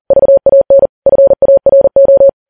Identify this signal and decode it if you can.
Morse code, heard as a keyed tone, about 40 wpm, 550 Hz. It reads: VAN FARO
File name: Files Web tls / Morse on9fv / morse40kps